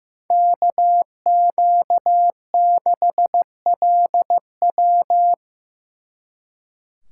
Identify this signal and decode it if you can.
Morse code, heard as a keyed tone, about 15 wpm, 700 Hz. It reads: KQ6LW
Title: Speed Racing - USA Call Signs 15wpm